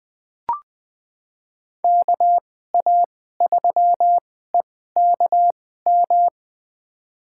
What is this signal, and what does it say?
Morse code, heard as a keyed tone, about 20 wpm, spaced as for 14 wpm, 700 Hz. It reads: KA3EKM
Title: USA Call Signs - 6 Characters - 2x Character Spacing - 20wpm